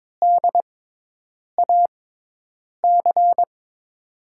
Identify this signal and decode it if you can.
Morse code, heard as a keyed tone, about 23 wpm, spaced as for 8 wpm, 700 Hz. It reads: DAC
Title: USA Call Signs - 6x Character Spacing - 22wpm